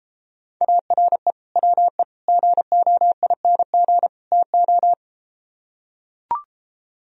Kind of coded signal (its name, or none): Morse code